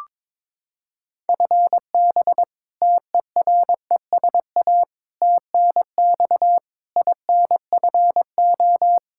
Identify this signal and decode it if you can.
Morse code, heard as a keyed tone, about 22 wpm, 700 Hz. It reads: FB TERESA TNX INFO
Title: Practice QSO - Exchange 3 - More Info - 22wpm